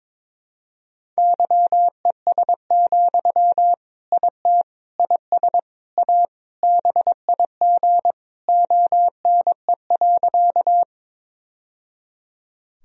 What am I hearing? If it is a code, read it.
Morse code, heard as a keyed tone, about 22 wpm, 700 Hz. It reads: YES, IT IS A BIG ONE.
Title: Sentences from Top 1000 Words - Review Part 1 - 22wpm